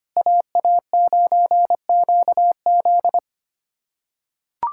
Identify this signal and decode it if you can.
Morse code, heard as a keyed tone, about 25 wpm, 700 Hz. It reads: AA9QZ
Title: USA Callsign DE Callsign - 5 Characters - 25wpm